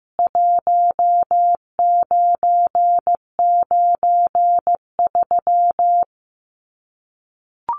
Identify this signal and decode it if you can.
Morse code, heard as a keyed tone, about 15 wpm, 700 Hz. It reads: 1993
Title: CWT Exchange 15wpm